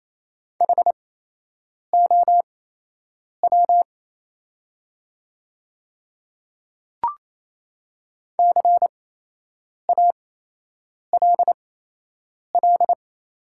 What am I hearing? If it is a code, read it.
Morse code, heard as a keyed tone, about 28 wpm, spaced as for 8 wpm, 690 Hz. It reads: HOW CALL
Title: 8x Character Spacing - Top 200 Words - 28wpm